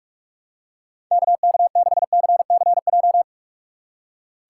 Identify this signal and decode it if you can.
Morse code, heard as a keyed tone, about 46 wpm, 700 Hz. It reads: KK6CCJ
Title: USA Call Signs - 6 Characters - 4x Character Spacing - 45wpm